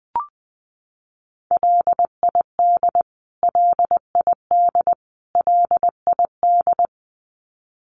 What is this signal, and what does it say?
Morse code, heard as a keyed tone, about 19 wpm, 700 Hz. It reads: LID LID LID
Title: Abbreviations Repeated 3x - 20wpm